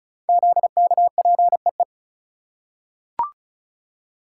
Morse code, heard as a keyed tone, about 36 wpm, 700 Hz. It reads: ZKPEE
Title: Code Groups - 5 Letters - 35wpm